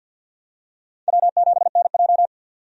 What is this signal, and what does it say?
Morse code, heard as a keyed tone, about 51 wpm, 700 Hz. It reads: W7NJ
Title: USA Call Signs - 4 Characters - 50wpm